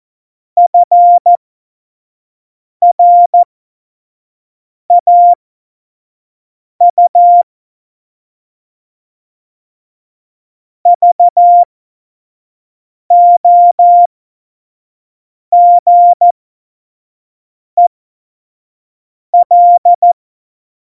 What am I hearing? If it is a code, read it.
Morse code, heard as a keyed tone, about 14 wpm, spaced as for 5 wpm, 700 Hz. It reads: FRAU VOGEL
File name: Morsekurs / Witze / Witz 08. September 2020, 25 F70